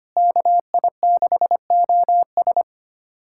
Morse code, heard as a keyed tone, about 25 wpm, 700 Hz. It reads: KI6OS
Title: USA Callsign DE Callsign - 5 Characters - 25wpm